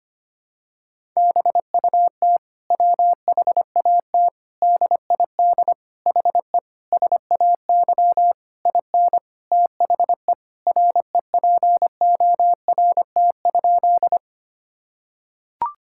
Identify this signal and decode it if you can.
Morse code, heard as a keyed tone, about 25 wpm, 700 Hz. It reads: BUT WHAT DID HE SAY IN THE REPORT?
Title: Sentences from Top 1000 Words - Review Part 6 - 25wpm